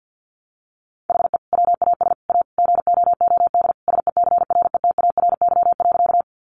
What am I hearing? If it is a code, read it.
Morse code, heard as a keyed tone, about 50 wpm, 700 Hz. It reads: HE WAS A GOOD SECRETARY.
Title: Sentences from Top 1300 Words 50wpm